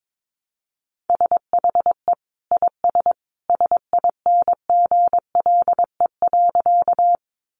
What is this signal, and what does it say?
Morse code, heard as a keyed tone, about 22 wpm, 700 Hz. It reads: SHE IS SINGLE.
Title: Sentences from Top 700 Words 22wpm